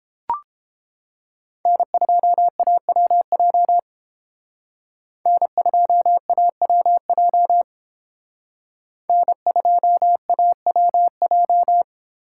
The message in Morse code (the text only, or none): N2AWJ N2AWJ N2AWJ